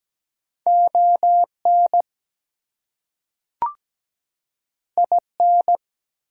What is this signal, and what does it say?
Morse code, heard as a keyed tone, about 17 wpm, 700 Hz. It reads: ON IN
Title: Two Letter Words - 17wpm